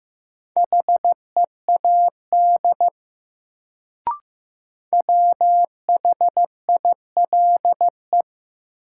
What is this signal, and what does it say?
Morse code, heard as a keyed tone, about 15 wpm, 700 Hz. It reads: HEAD WHILE
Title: Sentences from Top 300 Words - New Word Drill - 15wpm